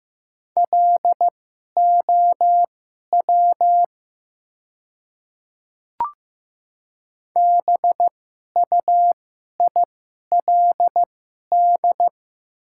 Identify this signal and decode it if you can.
Morse code, heard as a keyed tone, about 15 wpm, spaced as for 11 wpm, 700 Hz. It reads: LOW BUILD
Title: 2x Character Spacing - Top 300 Words - 15wpm